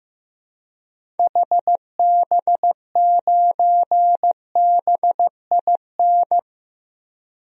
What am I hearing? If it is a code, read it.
Morse code, heard as a keyed tone, about 15 wpm, 700 Hz. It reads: HB9BIN
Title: DXCC-100 Call Signs - 6 Characters - 15wpm